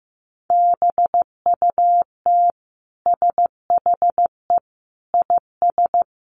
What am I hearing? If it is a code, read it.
Morse code, heard as a keyed tone, about 15 wpm, 700 Hz. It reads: BUT SHE IS
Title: Sentences from Top 1500 Words 15wpm